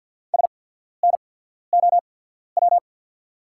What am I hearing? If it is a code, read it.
Morse code, heard as a keyed tone, about 48 wpm, spaced as for 13 wpm, 690 Hz. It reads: SNOW
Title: 8x Character Spacing - Top 500 Words - 50wpm